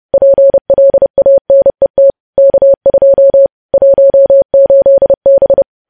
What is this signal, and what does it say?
Morse code, heard as a keyed tone, about 30 wpm, 550 Hz. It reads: PLANET K2 18B
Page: news headlines in morse code at 30 wpm